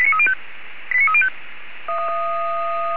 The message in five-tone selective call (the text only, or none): 89305